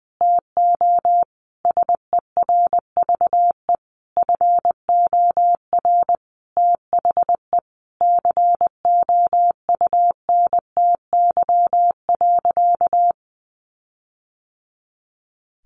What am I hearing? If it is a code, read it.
Morse code, heard as a keyed tone, about 20 wpm, 700 Hz. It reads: TO SERVE FOR THE COUNTY.
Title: Sentences from Top 1500 Words 20wpm